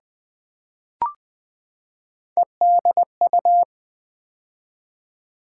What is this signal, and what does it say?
Morse code, heard as a keyed tone, about 20 wpm, 700 Hz. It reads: EDU
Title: Build A Word - 3 Letters At A Time - 8-Letter Words - 20wpm